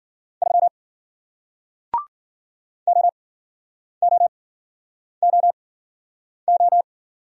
Morse code, heard as a keyed tone, about 53 wpm, spaced as for 9 wpm, 700 Hz. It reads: 3 OOOO